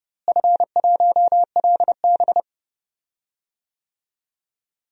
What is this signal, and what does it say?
Morse code, heard as a keyed tone, about 30 wpm, 700 Hz. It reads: F1LB
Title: DXCC-100 Call Signs - 4 Characters - Repeated 2x - 30wpm